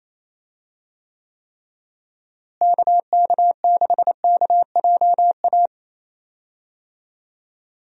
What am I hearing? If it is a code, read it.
Morse code, heard as a keyed tone, about 28 wpm, 700 Hz. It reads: KK6KJA
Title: USA Call Signs - 6 Characters - Repeated 3x - 28wpm